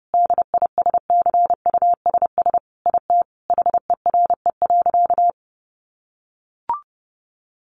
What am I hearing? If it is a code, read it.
Morse code, heard as a keyed tone, about 30 wpm, 700 Hz. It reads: DISCUSS IT HERE.
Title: Sentences from Top 1100 Words 30wpm